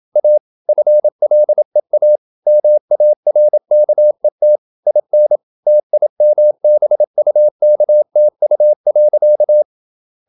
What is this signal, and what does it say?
Morse code, heard as a keyed tone, about 27 wpm, 600 Hz. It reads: A FLEA MARKET IN TIMBUKTU.
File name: Never 0027wpm